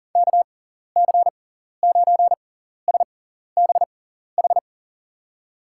Morse code, heard as a keyed tone, about 40 wpm, spaced as for 14 wpm, 700 Hz. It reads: KC9SBH
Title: USA Call Signs - 6 Characters - 6x Character Spacing - 40wpm